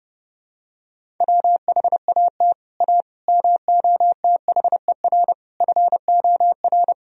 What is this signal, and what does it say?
Morse code, heard as a keyed tone, about 32 wpm, 700 Hz. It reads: WHAT A MOTHER FOR
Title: Sentences from Top 1000 Words - Review Part 1 - 30wpm